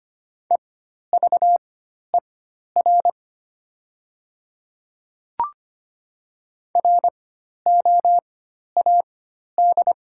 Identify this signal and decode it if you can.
Morse code, heard as a keyed tone, about 25 wpm, spaced as for 12 wpm, 700 Hz. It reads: EVER ROAD
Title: 4x Character Spacing - Top 500 Words - 25wpm